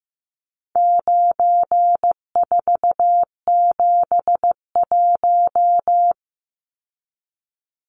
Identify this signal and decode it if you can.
Morse code, heard as a keyed tone, about 15 wpm, 700 Hz. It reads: 9471
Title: Numbers - 4-Digit 15wpm